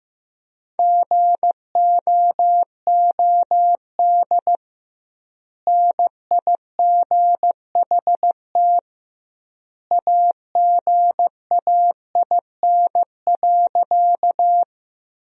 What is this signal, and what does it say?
Morse code, heard as a keyed tone, about 16 wpm, 700 Hz. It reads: GOOD NIGHT AGAIN.
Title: 2x Word-Spacing - Sentences from Top 300 Words - 15wpm